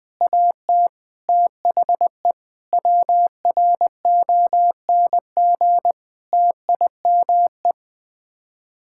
Morse code, heard as a keyed tone, about 20 wpm, 700 Hz. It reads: AT THE WRONG TIME
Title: Sets of 9 Words - 20wpm